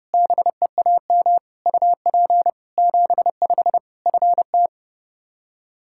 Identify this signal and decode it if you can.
Morse code, heard as a keyed tone, about 30 wpm, 700 Hz. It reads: BEAM UP 75 FT
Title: K1USN SST QSO Practice 30wpm